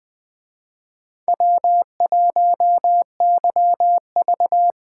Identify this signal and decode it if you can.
Morse code, heard as a keyed tone, about 20 wpm, 700 Hz. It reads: W1YV